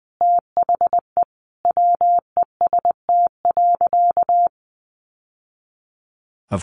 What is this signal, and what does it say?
Morse code, heard as a keyed tone, about 20 wpm, 700 Hz. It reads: THE WEST.
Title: Sentences from Top 1200 Words 20wpm